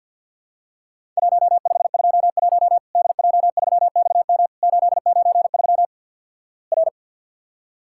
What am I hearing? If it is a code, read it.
Morse code, heard as a keyed tone, about 50 wpm, 700 Hz. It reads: 1521 DJ3XM 893 R
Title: Worked All Europe DX Contest - QTC Exchange - 50wpm